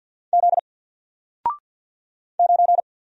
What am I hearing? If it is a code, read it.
Morse code, heard as a keyed tone, about 50 wpm, 700 Hz. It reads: Z 9
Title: Single Letter-Number 50wpm